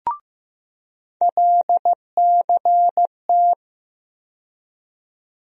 Morse code, heard as a keyed tone, about 15 wpm, 700 Hz. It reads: LCT
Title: USA Call Sign Postfixes - 3 Letters 15wpm